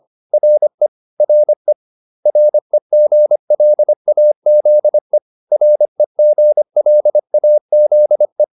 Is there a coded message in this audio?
Yes: Morse code from 0.3 to 8.4 s, heard as a keyed tone, about 25 wpm, 600 Hz. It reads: RE RE REGLAZE REGLAZE